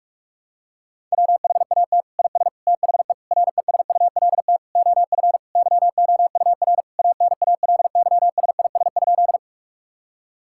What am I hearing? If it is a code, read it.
Morse code, heard as a keyed tone, about 46 wpm, 700 Hz. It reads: WHAT IS THE RESULT OF YOUR ANALYSIS?
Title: Sentences from Top 1000 Words - Review Part 5 - 45wpm